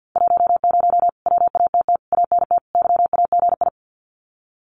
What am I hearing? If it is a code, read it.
Morse code, heard as a keyed tone, about 48 wpm, 700 Hz. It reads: 10 WATT ANT YAGI
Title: Common QSO Phrases Encore 50wpm